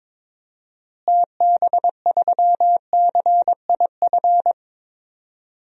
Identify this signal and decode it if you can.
Morse code, heard as a keyed tone, about 22 wpm, 700 Hz. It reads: TB3CIF
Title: DXCC-100 Call Signs - 6 Characters - Repeated 2x - 22wpm